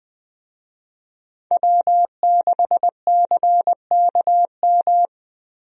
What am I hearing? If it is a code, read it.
Morse code, heard as a keyed tone, about 20 wpm, 700 Hz. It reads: W6CKM
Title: USA Call Signs - 5 Characters - Repeated 3x - 20wpm